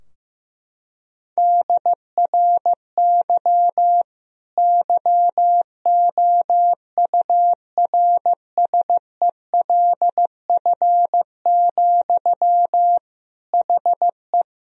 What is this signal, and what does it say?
Morse code, heard as a keyed tone, about 15 wpm, 700 Hz. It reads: DRY YOURSELF, HE
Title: Sentences from Top 1000 Words - Review Part 3 - 15wpm